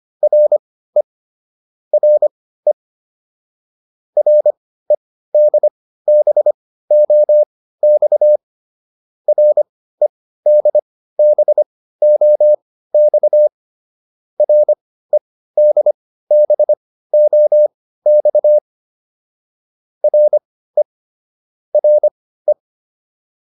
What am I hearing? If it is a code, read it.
Morse code, heard as a keyed tone, about 25 wpm, spaced as for 15 wpm, 600 Hz. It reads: RE RE REDBOX REDBOX REDBOX RE RE